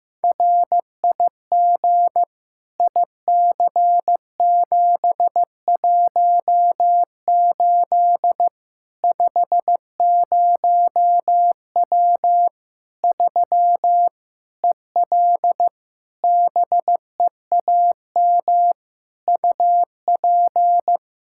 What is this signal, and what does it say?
Morse code, heard as a keyed tone, about 15 wpm, 700 Hz. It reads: RIG IC718 50W 3 EL BEAM UP
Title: Practice QSO - Exchange 2 - Info - 15wpm